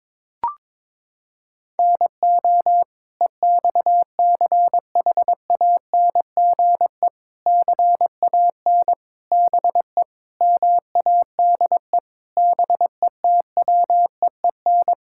Morse code, heard as a keyed tone, about 22 wpm, 700 Hz. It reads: NO EXCHANGE CAN BE MADE BETWEEN
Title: Sentences from Top 1000 Words - Review Part 6 - 22wpm